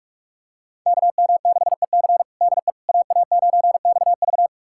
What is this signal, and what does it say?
Morse code, heard as a keyed tone, about 46 wpm, 700 Hz. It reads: KM6EC DE AA9XV
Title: USA Callsign DE Callsign - 5 Characters - 45wpm